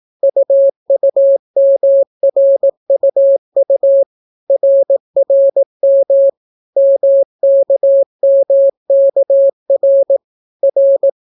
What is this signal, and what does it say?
Morse code, heard as a keyed tone, about 18 wpm, 550 Hz. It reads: UUMRUU RRM MKMKR R